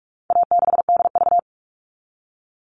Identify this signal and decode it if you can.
Morse code, heard as a keyed tone, about 44 wpm, 700 Hz. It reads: A6DV
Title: All Cut Numbers - 4-Digit 45wpm